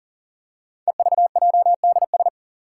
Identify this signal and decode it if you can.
Morse code, heard as a keyed tone, about 41 wpm, 700 Hz. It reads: EVJDS
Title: Code Groups - 5 Letters - 40wpm